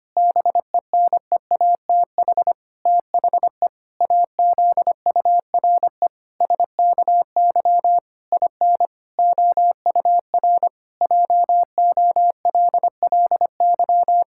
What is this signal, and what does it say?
Morse code, heard as a keyed tone, about 25 wpm, 700 Hz. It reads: BENEATH THE AZURE SKY IN OUR JOLLY
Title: Pangram Challenge v2 - 25wpm